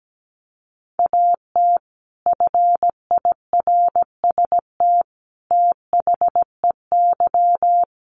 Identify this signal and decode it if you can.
Morse code, heard as a keyed tone, about 17 wpm, 700 Hz. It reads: AT FIRST THEY
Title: Sentences from Top 600 Words 17wpm